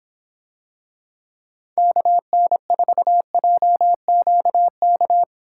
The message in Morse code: KN4JQK

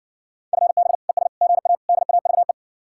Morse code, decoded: FB ES GA DAVE

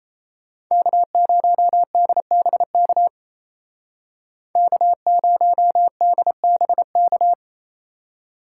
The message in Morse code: K0DBK K0DBK

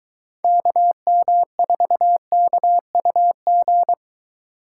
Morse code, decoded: KM4KUG

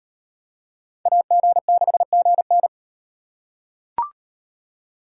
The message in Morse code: AG6GN

